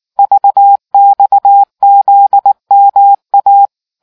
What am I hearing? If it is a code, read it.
Morse code, heard as a keyed tone, about 18 wpm, 800 Hz. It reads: VXZMA